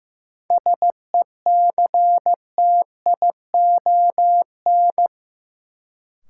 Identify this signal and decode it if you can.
Morse code, heard as a keyed tone, about 15 wpm, 700 Hz. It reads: SECTION
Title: Sentences from Top 1000 Words - Review Part 5 - New Word Drill - 15wpm